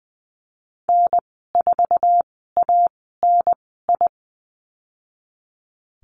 Morse code, heard as a keyed tone, about 20 wpm, spaced as for 14 wpm, 700 Hz. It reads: N4ANI
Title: 062 Lesson 11g - Letter h - Call Signs 20wpm